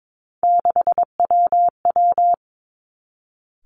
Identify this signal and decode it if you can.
Morse code, heard as a keyed tone, about 22 wpm, 700 Hz. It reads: 6WW
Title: Three Challenging Letter-Number Combinations 22wpm